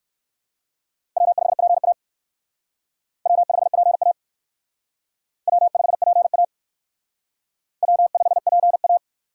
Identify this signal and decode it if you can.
Morse code, heard as a keyed tone, about 57 wpm, 700 Hz. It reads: W5PA W5PA W5PA W5PA